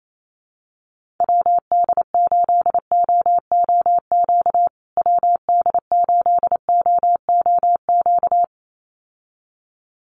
Morse code, heard as a keyed tone, about 28 wpm, 700 Hz. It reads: WD8OOQ WD8OOQ
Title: USA Call Signs Repeated 2x 28wpm